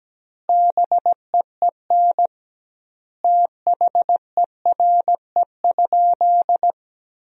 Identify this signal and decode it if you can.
Morse code, heard as a keyed tone, about 17 wpm, 700 Hz. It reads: BEEN THERE?
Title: 2x Word-Spacing - Sentences from Top 100 Words - 17wpm